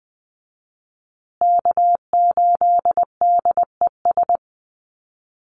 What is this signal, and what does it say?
Morse code, heard as a keyed tone, about 20 wpm, 700 Hz. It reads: K8DES
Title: USA Call Signs Repeated 3x 20wpm